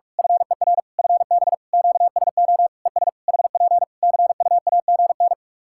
Morse code, heard as a keyed tone, about 41 wpm, 700 Hz. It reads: FER FB QSO ES HP CUAGN
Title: K1USN SST QSO Practice 45wpm